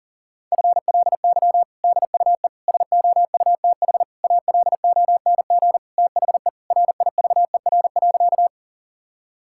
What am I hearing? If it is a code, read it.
Morse code, heard as a keyed tone, about 40 wpm, 700 Hz. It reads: FLY DUE SOUTH ALONG THE RIVER.
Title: Sentences from Top 1000 Words - Review Part 6 - 40wpm